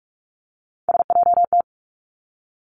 Morse code, heard as a keyed tone, about 46 wpm, 700 Hz. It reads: SJT
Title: Three Challenging Letter-Number Combinations 45wpm